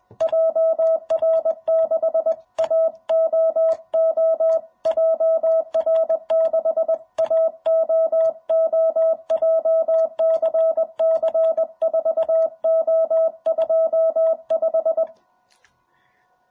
Morse code, heard as a keyed tone, about 21 wpm, 660 Hz. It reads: JR6AOO JR6AOO JCC 4O25
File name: jh8cbh / 2021cw